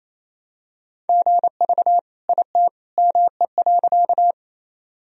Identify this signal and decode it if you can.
Morse code, heard as a keyed tone, about 28 wpm, 700 Hz. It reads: GV IT ME.
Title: Sentences from Top 200 Words and Abbreviations 28wpm